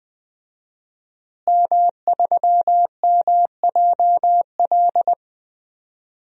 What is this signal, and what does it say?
Morse code, heard as a keyed tone, about 21 wpm, 700 Hz. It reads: M3MJL
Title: DXCC-100 Call Signs - 5 Characters - Repeated 2x - 20wpm